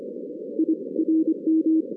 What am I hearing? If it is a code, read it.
Morse code, heard as a keyed tone, about 26 wpm, 320 Hz. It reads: I RM
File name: 2016 arrl dx cw / 2016-ARRL-DX-CW-K5TR- - 0-42